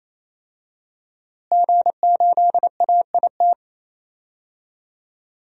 Morse code, heard as a keyed tone, about 27 wpm, 700 Hz. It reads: G8AIT
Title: DXCC-100 Call Signs - 5 Characters - Repeated 3x - 28wpm